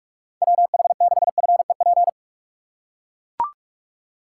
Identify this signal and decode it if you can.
Morse code, heard as a keyed tone, about 42 wpm, 700 Hz. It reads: WH6FEP
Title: USA Call Signs - 6 Characters - 4x Character Spacing - 45wpm